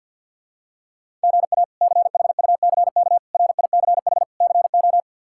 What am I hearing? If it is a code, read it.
Morse code, heard as a keyed tone, about 49 wpm, 700 Hz. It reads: GA CHUCK RICH CO
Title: K1USN SST QSO Practice - 50wpm